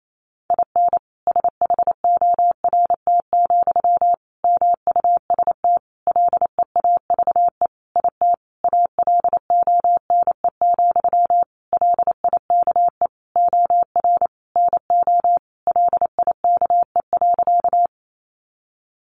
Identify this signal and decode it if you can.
Morse code, heard as a keyed tone, about 28 wpm, 700 Hz. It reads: IN SHORT, MUST LEAVE IT ALONE, LIKE OR NO LIKE.
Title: Sentences from Top 1000 Words 28wpm